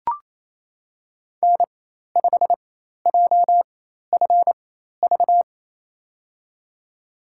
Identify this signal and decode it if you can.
Morse code, heard as a keyed tone, about 28 wpm, spaced as for 13 wpm, 700 Hz. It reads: N5JFV
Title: USA Call Signs - 5 Characters - 4x Character Spacing - 28wpm